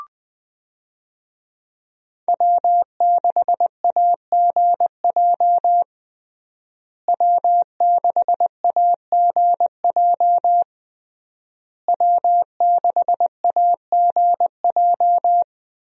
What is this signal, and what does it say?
Morse code, heard as a keyed tone, about 20 wpm, 700 Hz. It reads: W6AGJ W6AGJ W6AGJ
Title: USA Call Signs - 5 Characters - Repeated 3x - 20wpm